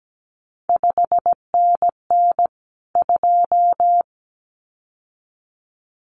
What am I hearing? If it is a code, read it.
Morse code, heard as a keyed tone, about 17 wpm, 700 Hz. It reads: 5NN 2
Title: CQ WW DX Contest Exchange - 17wpm